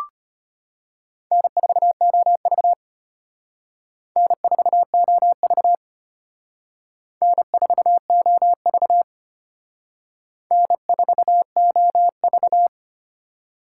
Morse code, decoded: N4OV N4OV N4OV N4OV

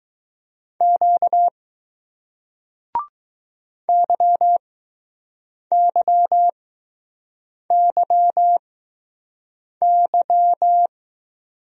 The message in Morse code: Q YYYY

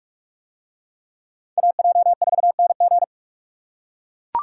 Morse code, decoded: AJ4NG

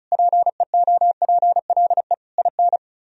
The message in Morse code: PEOPLE IN

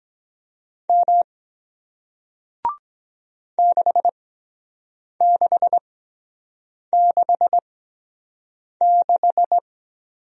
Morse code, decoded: M 6666

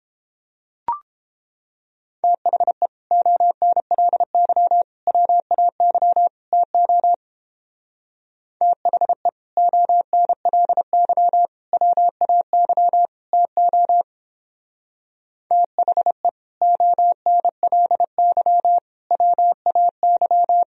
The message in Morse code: THE ONLY WAY TO THE ONLY WAY TO THE ONLY WAY

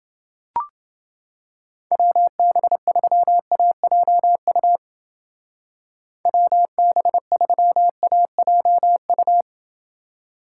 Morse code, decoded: WB3AJU WB3AJU

